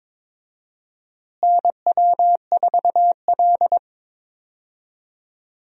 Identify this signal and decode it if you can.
Morse code, heard as a keyed tone, about 21 wpm, 700 Hz. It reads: NW4L
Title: USA Call Signs - 4 Characters - Repeated 3x - 22wpm